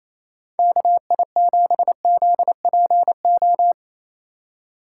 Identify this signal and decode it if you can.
Morse code, heard as a keyed tone, about 34 wpm, 700 Hz. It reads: KI7ZPO
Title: Speed Racing - USA Call Signs 28wpm